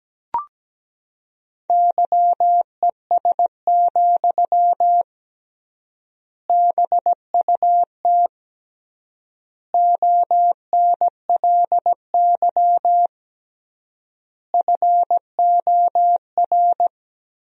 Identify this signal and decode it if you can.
Morse code, heard as a keyed tone, about 17 wpm, 700 Hz. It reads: YES, BUT ONLY FOR
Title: 3x Word-Spacing - Sentences from Top 500 Words - 17wpm